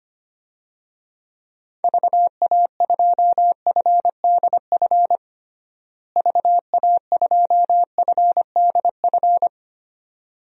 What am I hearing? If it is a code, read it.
Morse code, heard as a keyed tone, about 25 wpm, 700 Hz. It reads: VA2FDF VA2FDF
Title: DXCC-100 Call Signs - 6 Characters - Repeated 2x - 25wpm